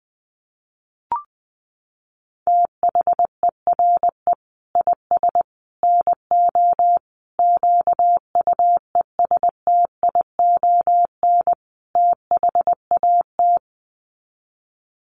Morse code, heard as a keyed tone, about 20 wpm, 700 Hz. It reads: THERE IS NO QUESTION THAT
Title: Sets of 5 Words Encore 20wpm